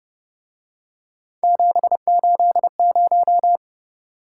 Morse code, heard as a keyed tone, about 29 wpm, 700 Hz. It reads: 780